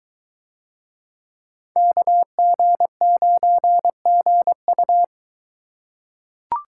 Morse code, heard as a keyed tone, about 24 wpm, 700 Hz. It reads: KG9GU